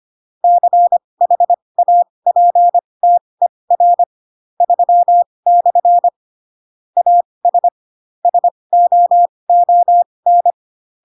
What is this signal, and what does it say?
Morse code, heard as a keyed tone, about 25 wpm, spaced as for 20 wpm, 700 Hz. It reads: CHAPTER 3/ AS SOON
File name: Pinocchio / Pinocchio ch3 15 25 ik1hge